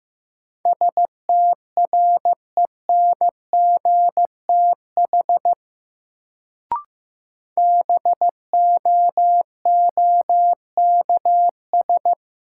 Morse code, heard as a keyed tone, about 15 wpm, 700 Hz. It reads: STRENGTH BOOKS
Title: Sentences from Top 1000 Words - Review Part 4 - New Word Drill - 15wpm